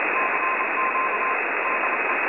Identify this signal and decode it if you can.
Morse code, heard as a keyed tone, about 27 wpm, 1040 Hz, with radio noise. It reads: CQ DE
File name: websdr recording 70175.0kHz